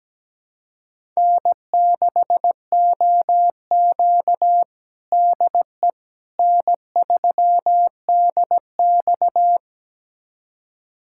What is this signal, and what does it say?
Morse code, heard as a keyed tone, about 17 wpm, 700 Hz. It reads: N6OQ DE N3DX
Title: USA Callsign DE Callsign - 4 Characters - 17wpm